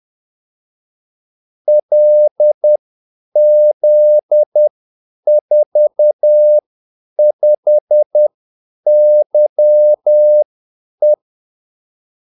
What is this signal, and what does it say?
Morse code, heard as a keyed tone, about 10 wpm, spaced as for 8 wpm, 600 Hz. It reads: LZ45YE
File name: Callsigns-10-8wpm